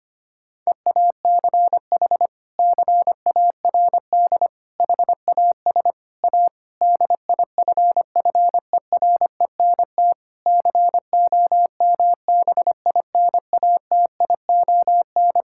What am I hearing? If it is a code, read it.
Morse code, heard as a keyed tone, about 25 wpm, 700 Hz. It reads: EACH CARD HAS A DIFFERENT COMBINATION
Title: Sentences from Top 1000 Words - Review Part 6 - 25wpm